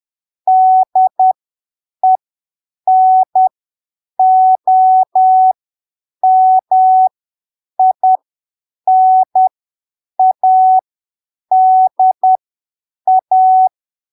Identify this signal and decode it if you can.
Morse code, heard as a keyed tone, about 10 wpm, spaced as for 7 wpm, 750 Hz. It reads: DENOMINADA
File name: arquivos / texto / texto 10 10ppm